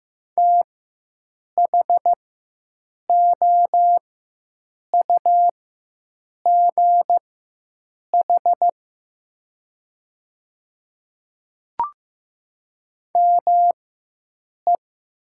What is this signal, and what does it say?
Morse code, heard as a keyed tone, about 15 wpm, spaced as for 7 wpm, 700 Hz. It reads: THOUGH ME
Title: 4x Character Spacing - Top 500 Words - 15wpm